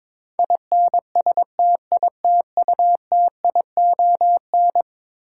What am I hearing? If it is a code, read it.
Morse code, heard as a keyed tone, about 22 wpm, 700 Hz. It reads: INSTITUTION
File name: practice / 22wpm / Eleven Letter Words - 22wpm